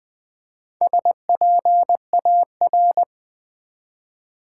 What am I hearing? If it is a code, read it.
Morse code, heard as a keyed tone, about 20 wpm, 700 Hz. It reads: SPAR
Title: Build A Word - 1 Letter At A Time - 5-Letter Words - 20wpm